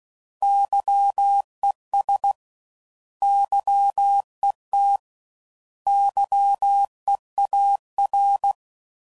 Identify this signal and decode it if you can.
Morse code, heard as a keyed tone, about 16 wpm, 800 Hz. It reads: YES YET YEAR
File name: Prefixes / By First Letter / Xs